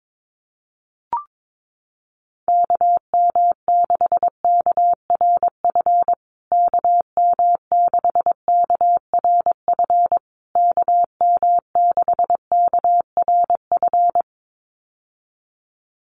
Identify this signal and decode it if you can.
Morse code, heard as a keyed tone, about 22 wpm, 690 Hz. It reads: KM6KRF KM6KRF KM6KRF
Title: USA Call Signs Repeated 3x 22wpm